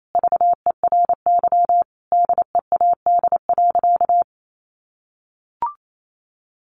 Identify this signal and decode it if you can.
Morse code, heard as a keyed tone, about 28 wpm, 700 Hz. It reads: VERY DEAD.
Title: A Sound Of Thunder 28wpm